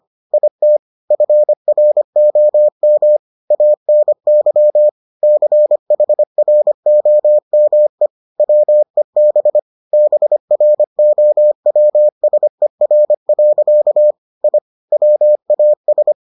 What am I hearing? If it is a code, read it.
Morse code, heard as a keyed tone, about 25 wpm, 600 Hz. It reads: IT FROM ANY CHROME WEB BROWSER. I WAS